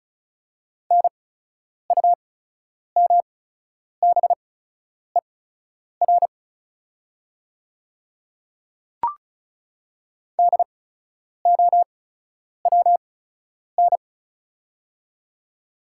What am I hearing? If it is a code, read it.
Morse code, heard as a keyed tone, about 34 wpm, spaced as for 10 wpm, 700 Hz. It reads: NUMBER DOWN
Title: 8x Character Spacing - Top 100 Words - 35wpm